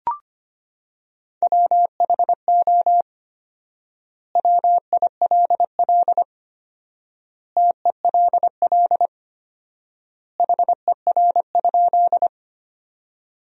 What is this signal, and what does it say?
Morse code, heard as a keyed tone, about 25 wpm, 700 Hz. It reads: WHO WILL TELL HER?
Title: 4x Word-Spacing - Sentences from Top 200 Words - 25wpm